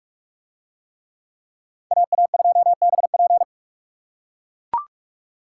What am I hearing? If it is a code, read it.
Morse code, heard as a keyed tone, about 44 wpm, 700 Hz. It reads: AA2BP